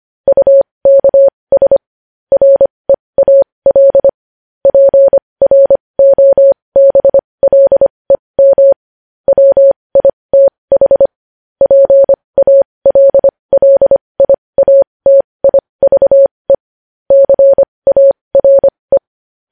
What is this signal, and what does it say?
Morse code, heard as a keyed tone, about 25 wpm, spaced as for 20 wpm, 550 Hz. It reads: UKS REAL PROBLEM WITH PALLIATIVE CARE
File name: podcast / News headlines-1x-25x20